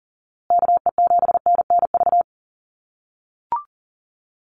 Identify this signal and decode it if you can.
Morse code, heard as a keyed tone, about 40 wpm, 700 Hz. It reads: KE7NNV
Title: USA Call Signs Repeated 2x 40wpm